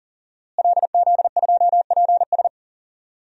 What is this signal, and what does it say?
Morse code, heard as a keyed tone, about 37 wpm, 700 Hz. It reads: LZ2PS
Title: DXCC-100 Call Signs - 5 Characters - Repeated 2x - 40wpm